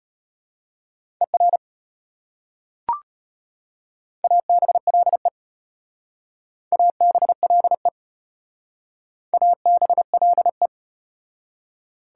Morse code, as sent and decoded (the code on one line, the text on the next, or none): . .-. / .- -... .-.. / .- -... .-.. / .- -... .-.. .
ER ABLE ABLE ABLE